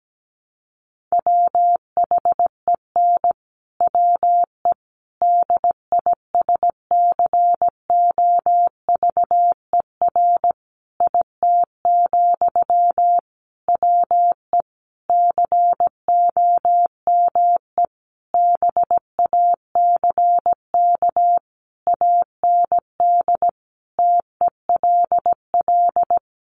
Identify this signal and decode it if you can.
Morse code, heard as a keyed tone, about 17 wpm, 700 Hz. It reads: WHEN WE DISCOVER IT, WE COME BACK AND TELL
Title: Sentences from Top 1100 Words 17wpm